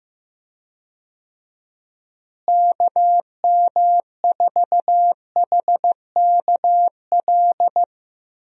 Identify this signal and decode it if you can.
Morse code, heard as a keyed tone, about 15 wpm, 700 Hz. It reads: KM4HKL
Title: USA Call Signs - 6 Characters - Repeated 2x - 15wpm